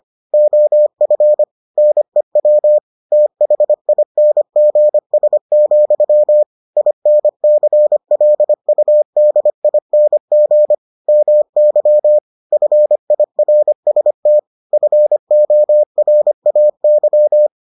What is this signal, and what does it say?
Morse code, heard as a keyed tone, about 25 wpm, 600 Hz. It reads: OF NEW THINGS, INCLUDING MY FIRST FORAY